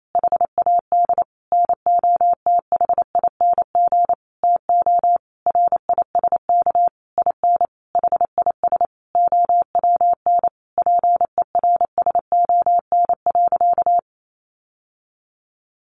Morse code, heard as a keyed tone, about 28 wpm, 700 Hz. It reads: HAD NOTHING TO RISK IN HIS OWN PERSON.
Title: Sentences from Top 600 Words 28wpm